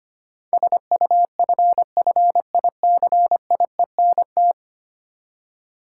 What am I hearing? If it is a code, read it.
Morse code, heard as a keyed tone, about 25 wpm, 700 Hz. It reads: SUFFICIENT
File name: Ten Letter Words - 25wpm